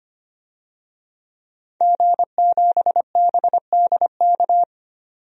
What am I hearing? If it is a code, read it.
Morse code, heard as a keyed tone, about 25 wpm, 700 Hz. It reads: G7BDK
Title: DXCC-100 Call Signs - 5 Characters - 25wpm